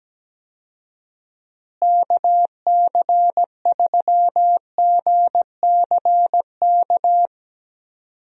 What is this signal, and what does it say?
Morse code, heard as a keyed tone, about 17 wpm, 700 Hz. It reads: KC3GCK
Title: USA Call Signs - 6 Characters - Repeated 3x - 17wpm